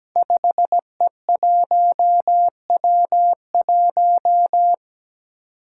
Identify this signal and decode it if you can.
Morse code, heard as a keyed tone, about 17 wpm, 700 Hz. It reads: 5E1W1
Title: Five Challenging Letter-Number Combinations 17wpm